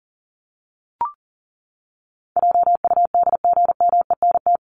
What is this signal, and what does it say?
Morse code, heard as a keyed tone, about 40 wpm, 700 Hz. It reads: JUDGMENT
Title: Top 1751-2000 Words 40wpm